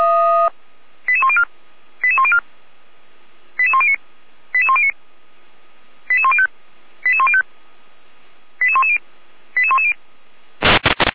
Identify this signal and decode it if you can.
Five-tone selective call (ZVEI), heard as quick runs of five tones, tones of about 70 ms. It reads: 88104 88104 88109 88109 88105 88105 88110 88110
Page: /455/SDR/ZVEI/88192/